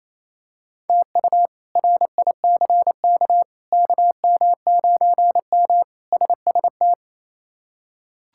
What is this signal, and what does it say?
Morse code, heard as a keyed tone, about 28 wpm, 700 Hz. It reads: TU RICK KM9M SST
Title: K1USN SST QSO Practice - 28wpm